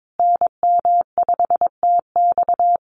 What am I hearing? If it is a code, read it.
Morse code, heard as a keyed tone, about 22 wpm, 700 Hz. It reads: NM5TX
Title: USA Call Signs Repeated 2x 22wpm